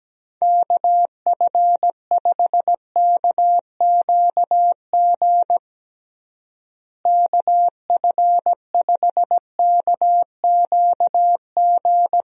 Morse code, heard as a keyed tone, about 17 wpm, 700 Hz. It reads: KF5KQG KF5KQG
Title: USA Call Signs - 6 Characters - Repeated 2x - 17wpm